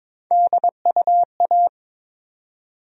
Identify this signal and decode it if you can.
Morse code, heard as a keyed tone, about 22 wpm, 700 Hz. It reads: DUA
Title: USA Call Sign Postfixes - 3 Letters 22wpm